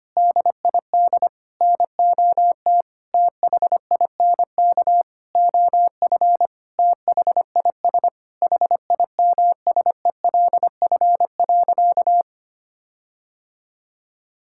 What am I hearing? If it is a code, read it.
Morse code, heard as a keyed tone, about 25 wpm, 700 Hz. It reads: DID NOT THINK OF THIS HIMSELF.
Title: Sentences from Top 1600 Words 25wpm